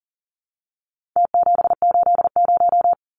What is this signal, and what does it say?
Morse code, heard as a keyed tone, about 40 wpm, 700 Hz. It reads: T780
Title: Numbers - Contest Style - 4-Digit 40wpm